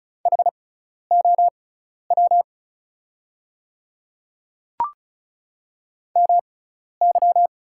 Morse code, heard as a keyed tone, about 36 wpm, spaced as for 12 wpm, 700 Hz. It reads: HOW MY
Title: 6x Character Spacing - Top 100 Words - 35wpm